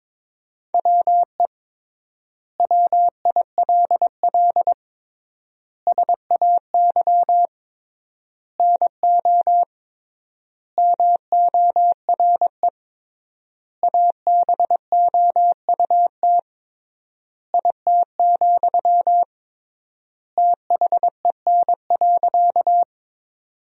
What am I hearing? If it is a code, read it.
Morse code, heard as a keyed tone, about 22 wpm, 700 Hz. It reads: WE WILL SAY NO MORE ABOUT IT, THEN.
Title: 3x Word-Spacing - Sentences from Top 200 Words - 22wpm